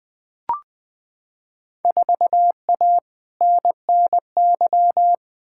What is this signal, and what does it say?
Morse code, heard as a keyed tone, about 20 wpm, 700 Hz. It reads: 4A NNY
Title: Field Day Exchanges - 20wpm